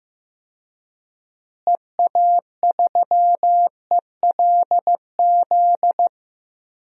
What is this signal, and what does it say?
Morse code, heard as a keyed tone, about 16 wpm, 700 Hz. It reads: EA3ELZ
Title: DXCC-100 Call Signs - 6 Characters - Repeated 2x - 15wpm